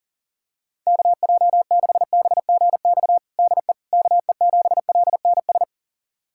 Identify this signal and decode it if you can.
Morse code, heard as a keyed tone, about 40 wpm, 700 Hz. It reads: KJ6BGX DE KE7LNS
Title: USA Callsign DE Callsign - 6 Characters - 40wpm